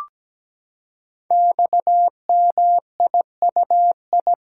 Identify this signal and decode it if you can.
Morse code, heard as a keyed tone, about 17 wpm, 700 Hz. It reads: XMIUI
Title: Code Groups - 5 Letters - 17wpm